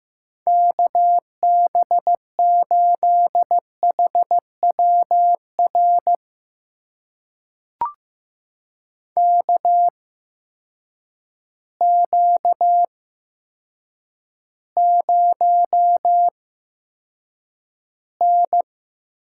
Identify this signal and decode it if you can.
Morse code, heard as a keyed tone, about 15 wpm, spaced as for 8 wpm, 700 Hz. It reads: KB8HWR KQ0N
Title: USA Call Signs - 8x Character Spacing - 15wpm